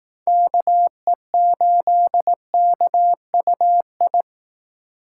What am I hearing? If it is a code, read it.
Morse code, heard as a keyed tone, about 19 wpm, 700 Hz. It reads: KE8KUI
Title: Speed Racing - USA 2x3 Call Signs 15wpm